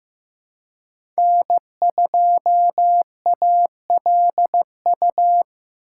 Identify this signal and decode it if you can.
Morse code, heard as a keyed tone, about 15 wpm, 700 Hz. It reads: N2ALU
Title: USA Call Signs - 5 Characters - 4x Character Spacing - 15wpm